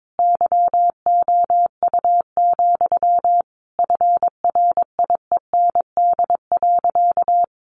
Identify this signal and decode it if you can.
Morse code, heard as a keyed tone, about 22 wpm, 700 Hz. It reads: YOU, FRIEND.
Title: Sentences from Top 1500 Words 22wpm